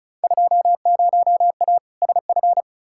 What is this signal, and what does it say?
Morse code, heard as a keyed tone, about 35 wpm, 700 Hz. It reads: 20A SF
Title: Field Day Exchanges Encore - 35wpm